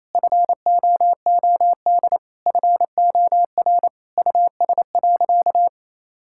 Morse code, heard as a keyed tone, about 28 wpm, 700 Hz. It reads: FOOD FOR US.
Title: Sentences from Top 1000 Words - Review Part 5 - 28wpm